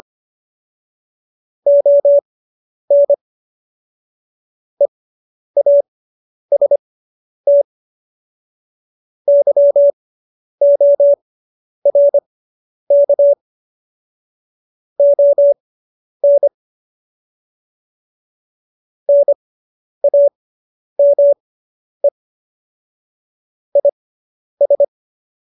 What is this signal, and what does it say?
Morse code, heard as a keyed tone, about 25 wpm, spaced as for 10 wpm, 570 Hz. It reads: ON EAST YORK ON NAME IS
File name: QSO220 10